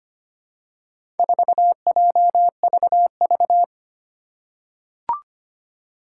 Morse code, heard as a keyed tone, about 25 wpm, 700 Hz. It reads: 4JVV
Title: Four Challenging Letter-Number Combinations 25wpm